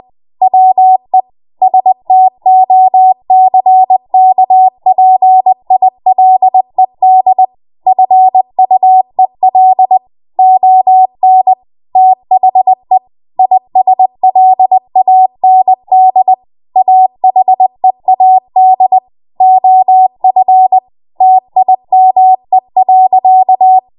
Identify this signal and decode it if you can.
Morse code, heard as a keyed tone, about 20 wpm, 750 Hz. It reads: WE STOCKPILED FUEL ON THE ISLAND AHEAD OF TIME.
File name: Morse / 20WPM